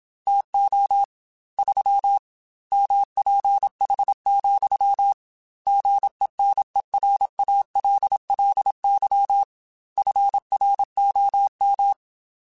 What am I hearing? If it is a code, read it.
Morse code, heard as a keyed tone, about 27 wpm, 800 Hz. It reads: TO 3 MPH, GENERALLY FROM
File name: Weather Sets / Weather Sets-mp3s / Weather 4 28 @25